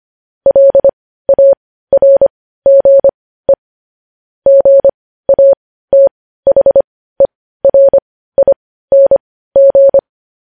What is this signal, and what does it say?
Morse code, heard as a keyed tone, about 25 wpm, spaced as for 15 wpm, 550 Hz. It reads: LARGE GATHERING
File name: podcast / News headlines-1x-25x15